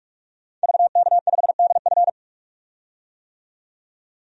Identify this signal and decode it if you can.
Morse code, heard as a keyed tone, about 47 wpm, 700 Hz. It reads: VK5DF
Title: DXCC-100 Call Signs - 5 Characters - 45wpm